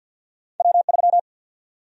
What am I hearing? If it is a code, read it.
Morse code, heard as a keyed tone, about 50 wpm, 700 Hz. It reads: W3
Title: USA Call Sign Prefixes - 50wpm